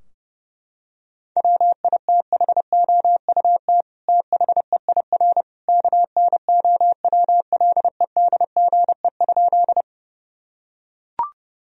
Morse code, heard as a keyed tone, about 32 wpm, 700 Hz. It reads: WITHOUT THEIR KNOWLEDGE?
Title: Sentences from Top 1000 Words - Review Part 3 - 30wpm